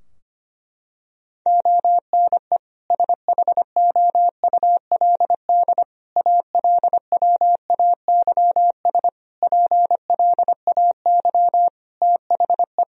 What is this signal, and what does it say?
Morse code, heard as a keyed tone, about 25 wpm, 700 Hz. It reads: ONE SHOULD ALWAYS PLAY THE
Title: Sentences from Top 1000 Words - Review Part 3 - 25wpm